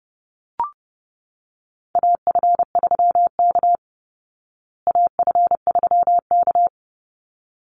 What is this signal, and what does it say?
Morse code, heard as a keyed tone, about 28 wpm, 700 Hz. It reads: AF3K AF3K
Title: Call Signs Repeated 3x - 30wpm